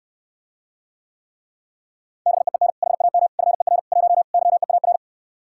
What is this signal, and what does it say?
Morse code, heard as a keyed tone, about 69 wpm, 700 Hz. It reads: BEEN HAM FER 1 YEAR